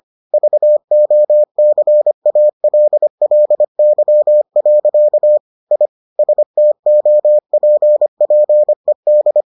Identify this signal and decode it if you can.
Morse code, heard as a keyed tone, about 25 wpm, 600 Hz. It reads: VOCALLY. I STOPPED